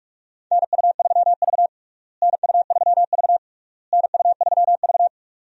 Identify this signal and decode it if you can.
Morse code, heard as a keyed tone, about 41 wpm, 700 Hz. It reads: NU3V NU3V NU3V
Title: USA Call Signs - 4 Characters - Repeated 3x - 45wpm